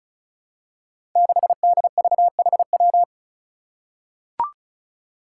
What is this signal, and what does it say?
Morse code, heard as a keyed tone, about 36 wpm, 700 Hz. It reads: 6DVHW
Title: Five Challenging Letter-Number Combinations 35wpm